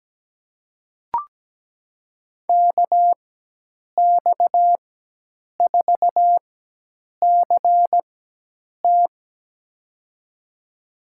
Morse code, heard as a keyed tone, about 17 wpm, spaced as for 8 wpm, 700 Hz. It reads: KX4CT
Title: USA Call Signs - 5 Characters - 4x Character Spacing - 17wpm